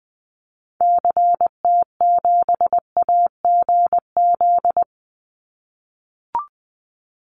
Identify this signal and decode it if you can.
Morse code, heard as a keyed tone, about 20 wpm, 700 Hz. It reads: CT7AGZ
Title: Call Signs - 20wpm